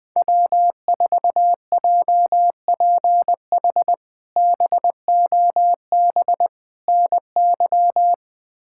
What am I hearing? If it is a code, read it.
Morse code, heard as a keyed tone, about 20 wpm, 700 Hz. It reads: W4JPH BOB NY
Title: K1USN SST QSO Practice - 20wpm